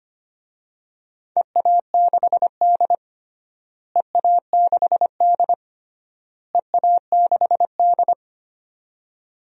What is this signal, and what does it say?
Morse code, heard as a keyed tone, about 25 wpm, 700 Hz. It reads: EA6D EA6D EA6D
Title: DXCC-100 Call Signs - 4 Characters - Repeated 3x - 25wpm